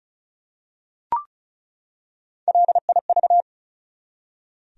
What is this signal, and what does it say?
Morse code, heard as a keyed tone, about 36 wpm, 700 Hz. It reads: LIV
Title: USA Common Names - Max 4 Letters - 35wpm